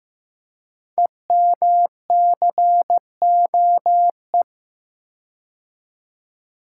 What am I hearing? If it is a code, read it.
Morse code, heard as a keyed tone, about 15 wpm, 700 Hz. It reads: EMCOE
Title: Code Groups - 5 Letter:Numbers - 15wpm